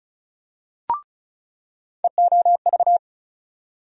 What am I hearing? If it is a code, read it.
Morse code, heard as a keyed tone, about 34 wpm, 700 Hz. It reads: EOV
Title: USA Call Sign Postfixes - 3 Letters 35wpm